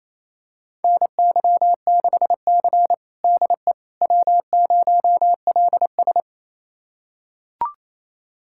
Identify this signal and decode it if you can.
Morse code, heard as a keyed tone, about 28 wpm, 700 Hz. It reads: NY6C DE W0LS
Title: USA Callsign DE Callsign - 4 Characters - 28wpm